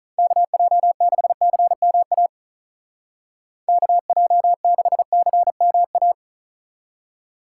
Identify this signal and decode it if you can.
Morse code, heard as a keyed tone, about 44 wpm, 700 Hz. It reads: KJ6CMA KJ6CMA